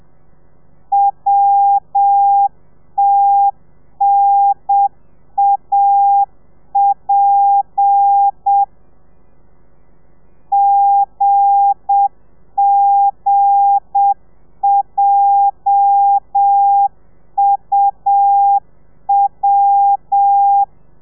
Morse code, heard as a keyed tone, about 7 wpm, 800 Hz. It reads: WTNAP GGJUW
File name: telegrafia / i2tfj / Lez8 30